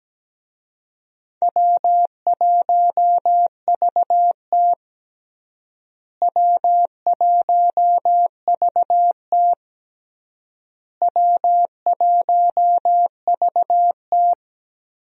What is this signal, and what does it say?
Morse code, heard as a keyed tone, about 17 wpm, 700 Hz. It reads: W1VT W1VT W1VT
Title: USA Call Signs - 4 Characters - Repeated 3x - 17wpm